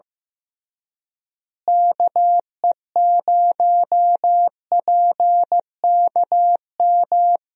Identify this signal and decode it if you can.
Morse code, heard as a keyed tone, about 15 wpm, 700 Hz. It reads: KE0PKM
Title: USA Call Signs - 6 Characters - Repeated 3x - 15wpm